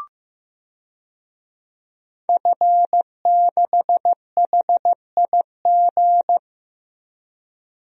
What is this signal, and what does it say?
Morse code, heard as a keyed tone, about 15 wpm, 700 Hz. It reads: F6HIG
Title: DXCC-100 Call Signs - 5 Characters - 15wpm